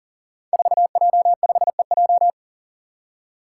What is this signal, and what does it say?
Morse code, heard as a keyed tone, about 40 wpm, 700 Hz. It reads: 4J5EJ
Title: Five Challenging Letter-Number Combinations 40wpm